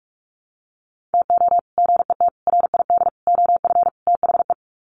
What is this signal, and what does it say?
Morse code, heard as a keyed tone, about 44 wpm, 700 Hz. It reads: TO GET RID OF THE
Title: Sets of 5 Words 45wpm